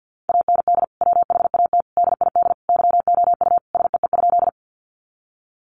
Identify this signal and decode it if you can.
Morse code, heard as a keyed tone, about 50 wpm, 700 Hz. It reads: AND WHAT DID YOU SEE?
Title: Sentences from Top 100 Words 50wpm